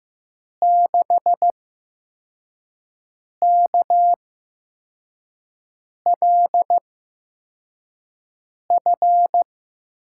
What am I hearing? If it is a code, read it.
Morse code, heard as a keyed tone, about 15 wpm, spaced as for 4 wpm, 700 Hz. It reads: 6KLF
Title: USA Call Signs - 8x Character Spacing - 15wpm